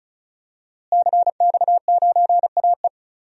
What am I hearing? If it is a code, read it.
Morse code, heard as a keyed tone, about 34 wpm, 700 Hz. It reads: CX9AE
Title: DXCC-100 Call Signs - 5 Characters - Repeated 2x - 35wpm